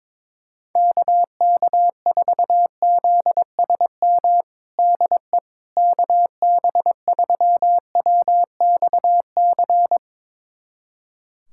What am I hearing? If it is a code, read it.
Morse code, heard as a keyed tone, about 22 wpm, 700 Hz. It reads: KK4ZSM DE KB3WXC
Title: USA Callsign DE Callsign - 6 Characters - 22wpm